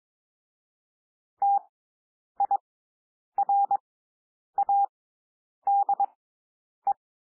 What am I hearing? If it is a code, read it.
Morse code, heard as a keyed tone, about 22 wpm, spaced as for 9 wpm, 800 Hz. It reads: TIRADE
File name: SOUND WORDS 22 wpm